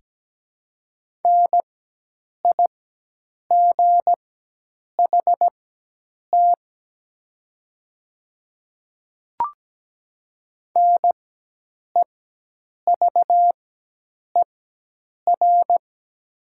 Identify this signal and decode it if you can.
Morse code, heard as a keyed tone, about 17 wpm, spaced as for 8 wpm, 700 Hz. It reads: NIGHT NEVER
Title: 4x Character Spacing - Top 300 Words - 17wpm